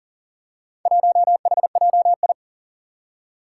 Morse code, heard as a keyed tone, about 43 wpm, 700 Hz. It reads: 1HJI
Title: Four Challenging Letter-Number Combinations 40wpm